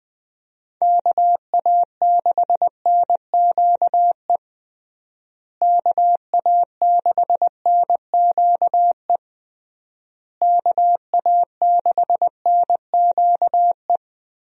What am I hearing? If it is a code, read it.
Morse code, heard as a keyed tone, about 20 wpm, 700 Hz. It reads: KA6NQE KA6NQE KA6NQE
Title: USA Call Signs - 6 Characters - Repeated 3x - 20wpm